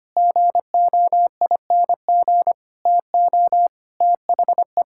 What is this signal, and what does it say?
Morse code, heard as a keyed tone, about 25 wpm, 700 Hz. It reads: GOING TO THE
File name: practice / 25wpm / Sentences with 4-Word Phrases - 25wpm